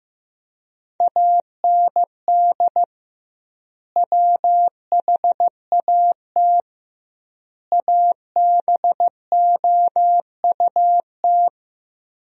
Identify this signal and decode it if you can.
Morse code, heard as a keyed tone, about 16 wpm, 700 Hz. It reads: AND WHAT ABOUT
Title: 2x Word-Spacing - Sentences from Top 200 Words - 15wpm